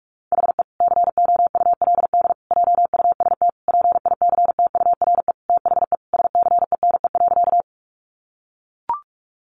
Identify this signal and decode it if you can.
Morse code, heard as a keyed tone, about 44 wpm, 700 Hz. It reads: HE COULD JUST PICTURE THE SCENE.
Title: Sentences from Top 1100 Words 45wpm